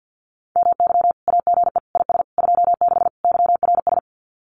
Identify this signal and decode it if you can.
Morse code, heard as a keyed tone, about 48 wpm, 700 Hz. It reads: MY AGE IS 26 YRS
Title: Common QSO Phrases Encore 50wpm